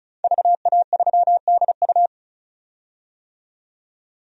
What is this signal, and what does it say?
Morse code, heard as a keyed tone, about 34 wpm, 700 Hz. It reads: VA3DU
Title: DXCC-100 Call Signs - 5 Characters - 35wpm